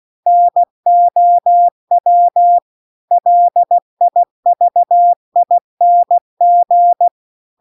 Morse code, heard as a keyed tone, about 16 wpm, 700 Hz. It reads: NOW LIVING
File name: CW-The-Big-Twoheaded-River-2-16WPM